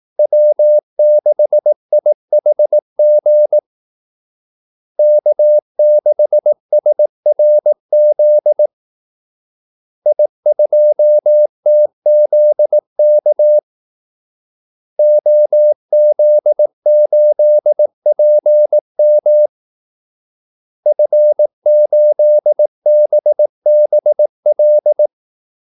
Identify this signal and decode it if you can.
Morse code, heard as a keyed tone, about 18 wpm, 600 Hz. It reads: W6IHG K6SRZ I2TZK OZ8PM F8BBL
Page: Kaldesignaler 18 wpm